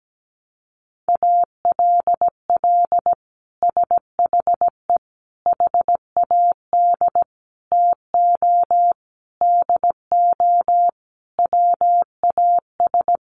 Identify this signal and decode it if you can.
Morse code, heard as a keyed tone, about 17 wpm, 700 Hz. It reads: ALL SHE HAD TO DO WAS
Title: Sentences from Top 1100 Words 17wpm